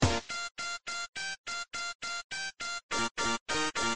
There is no code.